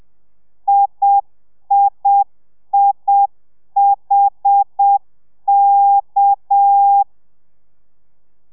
Morse code, heard as a keyed tone, about 7 wpm, 800 Hz. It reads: IIIHK